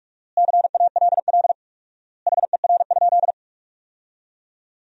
Morse code, heard as a keyed tone, about 42 wpm, 700 Hz. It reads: CALL HER?
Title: 4x Word-Spacing - Sentences from Top 100 Words - 45wpm